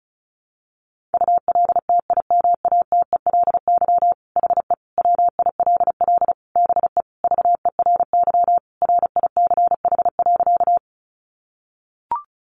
Morse code, heard as a keyed tone, about 34 wpm, 700 Hz. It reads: ULTIMATELY HE WILL BE VERY RICH.
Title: Sentences from Top 1200 Words 35wpm